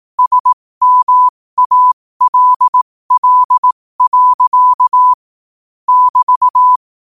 Morse code, heard as a keyed tone, about 18 wpm, spaced as for 15 wpm, 1000 Hz. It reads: SMALL. <BT>